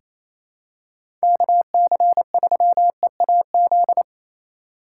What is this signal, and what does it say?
Morse code, heard as a keyed tone, about 28 wpm, 700 Hz. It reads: KC3EAZ
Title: USA Call Signs - 6 Characters - 4x Character Spacing - 28wpm